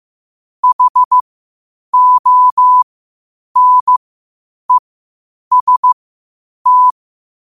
Morse code, heard as a keyed tone, about 15 wpm, spaced as for 8 wpm, 1000 Hz. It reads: HONEST
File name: qotdfiles / 8wpm